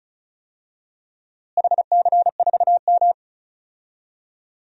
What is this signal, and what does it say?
Morse code, heard as a keyed tone, about 34 wpm, 700 Hz. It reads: HC4M
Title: DXCC-100 Call Signs - 4 Characters - Repeated 3x - 35wpm